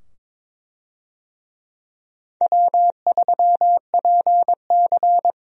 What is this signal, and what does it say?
Morse code, heard as a keyed tone, about 22 wpm, 700 Hz. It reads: W3PC
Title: USA Call Signs - 4 Characters - 22wpm